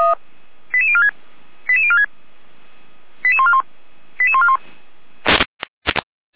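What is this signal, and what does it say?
Five-tone selective call (ZVEI), heard as quick runs of five tones, tones of about 70 ms. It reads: 88046 88046 88141 88141
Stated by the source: ZVEI five-tone selective call received on an SDR, used by a fire brigade.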